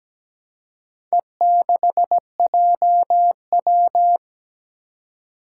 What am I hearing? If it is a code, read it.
Morse code, heard as a keyed tone, about 17 wpm, 700 Hz. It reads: E6JW
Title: Four Challenging Letter-Number Combinations 17wpm